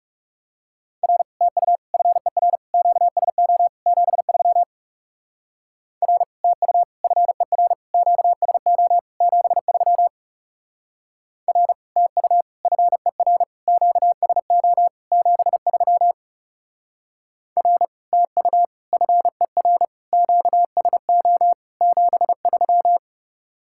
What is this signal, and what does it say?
Morse code, heard as a keyed tone, about 35 wpm, 700 Hz. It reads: R TU FER QSO 73 R TU FER QSO 73 R TU FER QSO 73 R TU FER QSO 73
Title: Speed-Racing Common QSO Phrases Encore - 30wpm